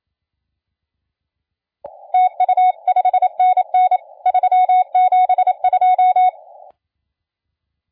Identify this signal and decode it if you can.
Morse code, heard as a keyed tone, about 27 wpm, 740 Hz. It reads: TU 5NN 372
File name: la4c / sac-cw-2017 / CONTEST AUDIO 1100Z